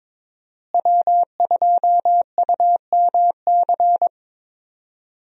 Morse code, heard as a keyed tone, about 21 wpm, 700 Hz. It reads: W2UMC
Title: USA Call Signs - 5 Characters - Repeated 3x - 22wpm